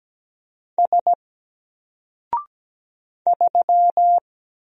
Morse code, heard as a keyed tone, about 17 wpm, 700 Hz. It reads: S 3
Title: Single Letter-Number Alt 17wpm